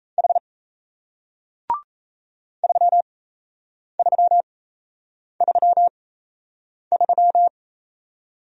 Morse code, heard as a keyed tone, about 37 wpm, spaced as for 8 wpm, 700 Hz. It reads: H 3333